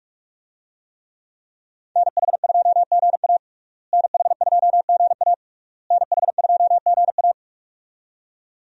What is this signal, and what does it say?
Morse code, heard as a keyed tone, about 46 wpm, 700 Hz. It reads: NH2GA NH2GA NH2GA
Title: USA Call Signs - 5 Characters - Repeated 3x - 45wpm